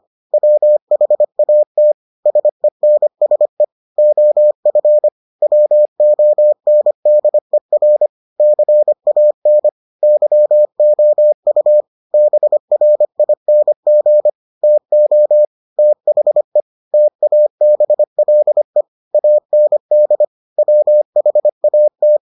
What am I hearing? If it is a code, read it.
Morse code, heard as a keyed tone, about 25 wpm, 600 Hz. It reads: WHAT SENSE OF WONDER CAN YOU BRING TO THE TABLE AND WHAT